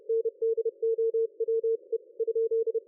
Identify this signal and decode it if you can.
Morse code, heard as a keyed tone, about 30 wpm, 450 Hz. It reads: NDOWE?
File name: dlazaawansowanych / nagraniazpasma / ExpertSDR2 line out 23.01.2020 14.22.53